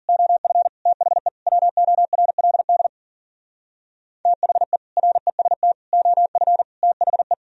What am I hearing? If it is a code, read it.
Morse code, heard as a keyed tone, about 51 wpm, 700 Hz. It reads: OF THE WORLD THE REST OF THE